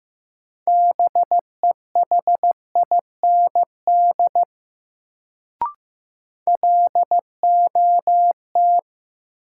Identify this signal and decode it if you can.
Morse code, heard as a keyed tone, about 15 wpm, 700 Hz. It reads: BEHIND LOT
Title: Sentences from Top 500 Words - New Word Drill - 15wpm